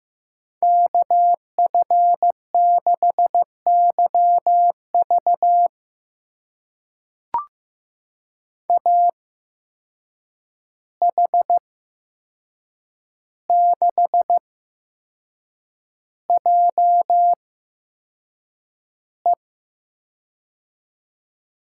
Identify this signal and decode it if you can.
Morse code, heard as a keyed tone, about 15 wpm, spaced as for 6 wpm, 700 Hz. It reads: KF6YV AH6JE
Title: USA Call Signs - 5 Characters - 8x Character Spacing - 15wpm